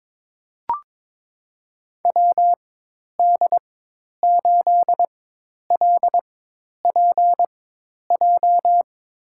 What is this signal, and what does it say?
Morse code, heard as a keyed tone, about 22 wpm, spaced as for 10 wpm, 700 Hz. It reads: WD8LPJ
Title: USA Call Signs - 6 Characters - 4x Character Spacing - 22wpm